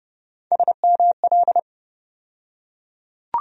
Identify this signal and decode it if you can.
Morse code, heard as a keyed tone, about 30 wpm, 700 Hz. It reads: SML
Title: USA Call Sign Postfixes - 3 Letters 30wpm